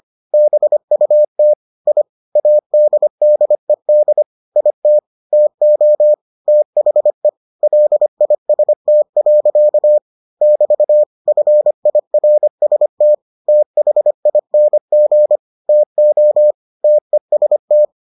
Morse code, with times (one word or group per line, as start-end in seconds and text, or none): 0.3-1.5 BUT
1.9-2.0 I
2.4-4.2 ADDED
4.6-5.0 IT
5.3-6.1 TO
6.5-7.3 THE
7.6-10.0 LIST.
10.4-11.0 <BT>
11.3-13.1 FIRST
13.5-15.4 THING
15.7-16.5 TO
16.8-17.9 TEST